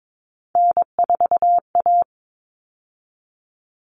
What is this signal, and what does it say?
Morse code, heard as a keyed tone, about 22 wpm, 700 Hz. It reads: N4A
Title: All Cut Numbers - 3-Digit 22wpm